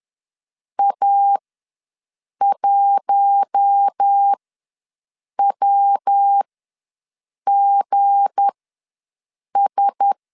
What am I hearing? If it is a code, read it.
Morse code, heard as a keyed tone, about 10 wpm, spaced as for 6 wpm, 800 Hz. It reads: A1WGS
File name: LEZIONI / 9 2867 ws 10wpm 4215